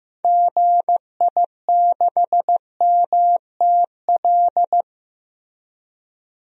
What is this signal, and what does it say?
Morse code, heard as a keyed tone, about 15 wpm, 700 Hz. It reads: GI6MTL
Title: DXCC-100 Call Signs - 6 Characters - Repeated 3x - 15wpm